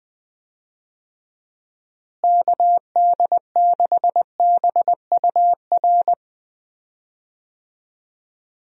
Morse code, heard as a keyed tone, about 20 wpm, 700 Hz. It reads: KD6BUR
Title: USA Call Signs - 6 Characters - Repeated 2x - 20wpm